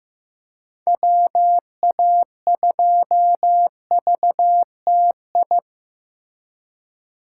Morse code, heard as a keyed tone, about 18 wpm, 700 Hz. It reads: WA2VTI
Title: Speed Racing - USA 2x3 Call Signs 15wpm